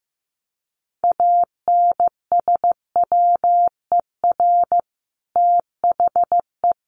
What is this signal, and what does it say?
Morse code, heard as a keyed tone, about 15 wpm, 700 Hz. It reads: ANSWER THE
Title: Sentences from Top 1100 Words 15wpm